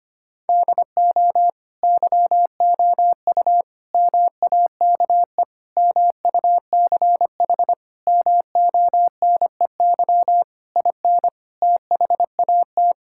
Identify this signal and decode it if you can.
Morse code, heard as a keyed tone, about 25 wpm, 700 Hz. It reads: DO YOU MAKE MUCH MONEY IN THAT
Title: Sentences from Top 1000 Words - Review Part 6 - 25wpm